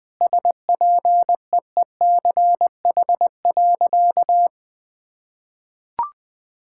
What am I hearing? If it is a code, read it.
Morse code, heard as a keyed tone, about 20 wpm, 700 Hz. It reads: SPEECH.
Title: Sentences with 5-Word Phrases - 20wpm